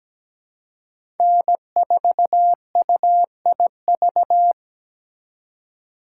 Morse code, heard as a keyed tone, about 17 wpm, 700 Hz. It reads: N4UIV
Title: USA Call Signs - 5 Characters - 2x Character Spacing - 17wpm